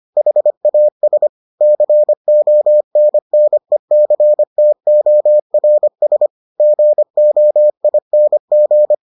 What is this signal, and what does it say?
Morse code, heard as a keyed tone, about 25 wpm, 600 Hz. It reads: HAS CONNECTORS GOING